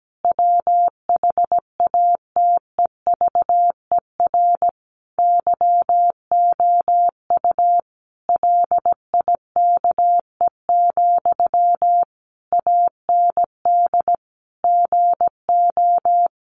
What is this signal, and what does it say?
Morse code, heard as a keyed tone, about 17 wpm, 700 Hz. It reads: WHATEVER YOU LIKE, AND GO
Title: Sentences from Top 1100 Words 17wpm